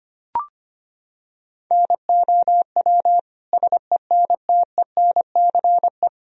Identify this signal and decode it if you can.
Morse code, heard as a keyed tone, about 25 wpm, 700 Hz. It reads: NOW SENTENCE
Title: Sentences from Top 1000 Words - Review Part 1 - 25wpm